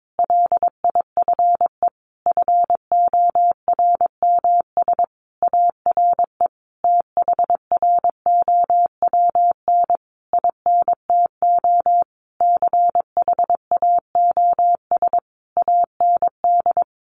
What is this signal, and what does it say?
Morse code, heard as a keyed tone, about 22 wpm, 700 Hz. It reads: LIFE FORMS ARE THROWN INTO CHAOS AND
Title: A Sound Of Thunder 22wpm